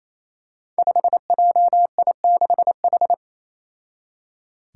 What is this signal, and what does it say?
Morse code, heard as a keyed tone, about 28 wpm, 700 Hz. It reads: 5JI6H
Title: Five Challenging Letter-Number Combinations 28wpm